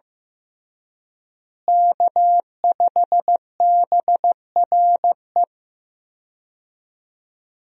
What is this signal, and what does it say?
Morse code, heard as a keyed tone, about 16 wpm, 700 Hz. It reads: K5BRE
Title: USA Call Signs - 5 Characters - Repeated 3x - 15wpm